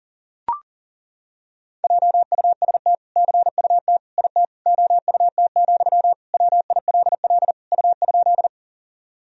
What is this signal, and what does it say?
Morse code, heard as a keyed tone, about 40 wpm, 700 Hz. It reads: JUST CUT IT OUT, WILL U?
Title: Sentences from Top 300 Words and Abbreviations 40wpm